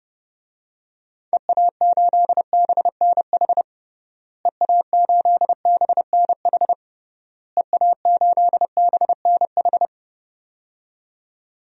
Morse code, heard as a keyed tone, about 32 wpm, 700 Hz. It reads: EA8BNH EA8BNH EA8BNH
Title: DXCC-100 Call Signs - 6 Characters - Repeated 3x - 30wpm